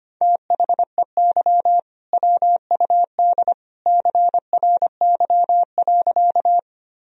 Morse code, heard as a keyed tone, about 25 wpm, 700 Hz. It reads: THEY WUD CRY.
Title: Sentences from Top 500 Words and Abbreviations 25wpm